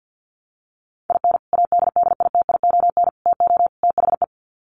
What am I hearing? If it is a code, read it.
Morse code, heard as a keyed tone, about 50 wpm, 700 Hz. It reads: IN ADDITION TO THE
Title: Sets of 4 Words 50wpm